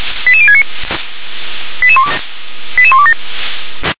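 Five-tone selective call (ZVEI), heard as quick runs of five tones, tones of about 70 ms. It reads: 88069 88127
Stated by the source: ZVEI five-tone selective call 88040 received by SDR.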